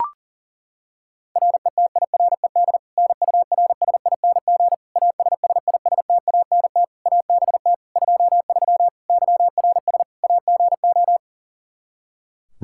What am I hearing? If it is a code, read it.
Morse code, heard as a keyed tone, about 40 wpm, 700 Hz. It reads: RETIRED NURSING ASSISTANT ABT 23 YRS AGO
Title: Practice QSO - Exchange 3 - More Info - 40wpm